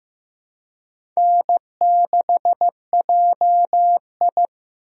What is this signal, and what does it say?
Morse code, heard as a keyed tone, about 15 wpm, 700 Hz. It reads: N6JI
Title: USA Call Signs - 4 Characters - Repeated 3x - 15wpm